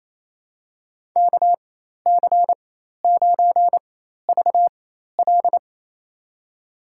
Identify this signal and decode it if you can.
Morse code, heard as a keyed tone, about 28 wpm, spaced as for 13 wpm, 700 Hz. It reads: KC9VL
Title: USA Call Signs - 5 Characters - 4x Character Spacing - 28wpm